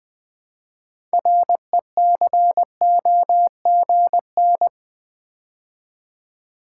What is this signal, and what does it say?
Morse code, heard as a keyed tone, about 20 wpm, 700 Hz. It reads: RECOGN
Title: Build A Word - 3 Letters At A Time - 9-Letter Words - 20wpm